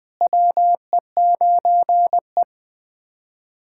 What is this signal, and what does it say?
Morse code, heard as a keyed tone, about 21 wpm, 700 Hz. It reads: WE9E
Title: USA Callsign DE Callsign - 4 Characters - 20wpm